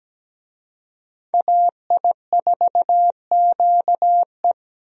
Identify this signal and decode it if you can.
Morse code, heard as a keyed tone, about 17 wpm, 700 Hz. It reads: AI4QE
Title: USA Call Signs - 5 Characters - Repeated 3x - 17wpm